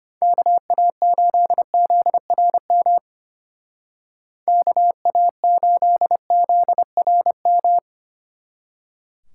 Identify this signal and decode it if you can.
Morse code, heard as a keyed tone, about 31 wpm, 700 Hz. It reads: KA8ZRM KA8ZRM
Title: Speed Racing - USA 2x3 Call Signs 25wpm